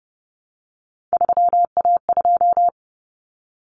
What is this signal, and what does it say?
Morse code, heard as a keyed tone, about 30 wpm, 700 Hz. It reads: 3A2
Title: Common Cut Numbers - 3-Digit 30wpm